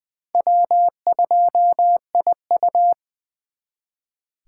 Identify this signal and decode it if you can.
Morse code, heard as a keyed tone, about 23 wpm, 700 Hz. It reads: W2IU
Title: Speed Racing - USA Call Signs 20wpm